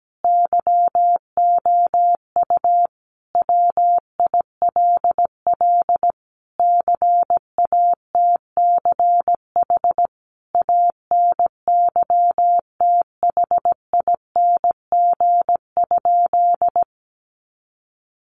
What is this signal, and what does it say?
Morse code, heard as a keyed tone, about 17 wpm, 700 Hz. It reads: YOU WILL CATCH ANYTHING?
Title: Sentences from Top 1500 Words 17wpm